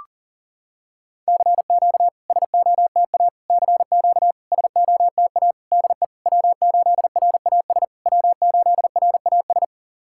Morse code, heard as a keyed tone, about 40 wpm, 700 Hz. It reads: CQ SOTA CQ SOTA DE W8RAS W8RAS
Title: SOTA QSO Practice 40wpm